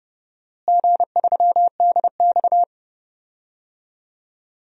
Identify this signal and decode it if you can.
Morse code, heard as a keyed tone, about 30 wpm, 700 Hz. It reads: G3DX
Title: DXCC-100 Call Signs - 4 Characters - Repeated 3x - 30wpm